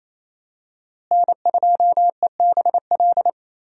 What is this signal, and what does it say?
Morse code, heard as a keyed tone, about 27 wpm, 700 Hz. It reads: N2EBL
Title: USA Call Signs - 5 Characters - Repeated 2x - 28wpm